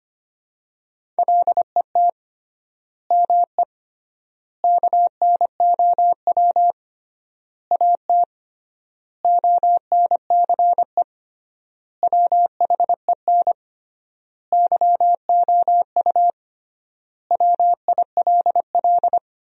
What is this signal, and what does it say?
Morse code, heard as a keyed tone, about 25 wpm, 700 Hz. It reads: LET ME KNOW AT ONCE WHEN YOU WILL
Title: 3x Word-Spacing - Sentences from Top 300 Words - 25wpm